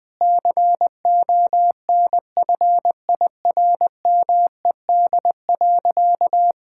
Morse code, heard as a keyed tone, about 20 wpm, 700 Hz. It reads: CONFIRMED.
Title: Sentences with 8-Word Phrases - 20wpm